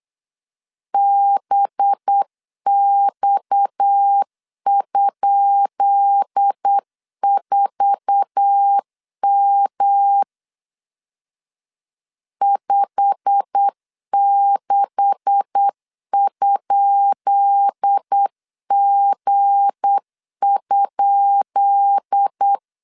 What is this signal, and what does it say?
Morse code, heard as a keyed tone, about 8 wpm, 800 Hz. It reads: BX?4M 56?G?
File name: LEZIONI / 10 SIGNS / 10B INT 8wpm 3534